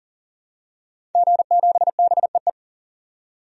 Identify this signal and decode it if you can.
Morse code, heard as a keyed tone, about 40 wpm, 700 Hz. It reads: G7BEE
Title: DXCC-100 Call Signs - 5 Characters - 40wpm